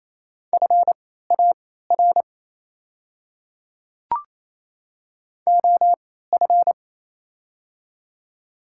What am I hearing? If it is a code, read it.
Morse code, heard as a keyed tone, about 28 wpm, spaced as for 16 wpm, 700 Hz. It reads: FAR OF
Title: 3x Character Spacing - Top 300 Words - 28wpm